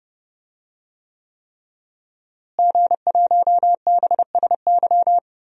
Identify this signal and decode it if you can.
Morse code, heard as a keyed tone, about 29 wpm, 700 Hz. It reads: G1BSY
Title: DXCC-100 Call Signs - 5 Characters - Repeated 3x - 30wpm